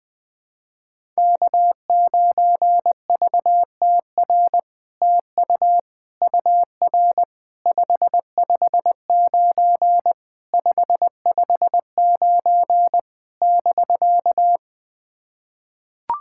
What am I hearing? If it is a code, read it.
Morse code, heard as a keyed tone, about 20 wpm, 700 Hz. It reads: K9VTR TU UR 559 559 <BK>
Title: SOTA QSO Practice 20wpm